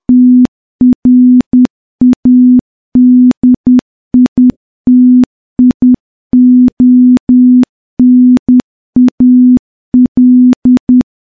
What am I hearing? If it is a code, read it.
Morse code, heard as a keyed tone, about 10 wpm, 250 Hz. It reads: TRADITIONAL
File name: morse-training-10WPM